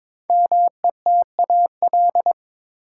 Morse code, heard as a keyed tone, about 22 wpm, 700 Hz. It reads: METAL
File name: 5 Letter Words - First 3 Letters Given - 22wpm